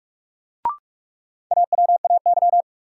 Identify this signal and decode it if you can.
Morse code, heard as a keyed tone, about 42 wpm, 700 Hz. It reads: AWAY
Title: Sets of 2 Words - New Word Drill - 45wpm